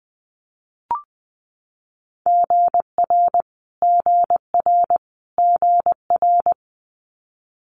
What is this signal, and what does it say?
Morse code, heard as a keyed tone, about 20 wpm, 700 Hz. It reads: GR GR GR
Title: Multi-letter Phonemes Repeated 3x 20wpm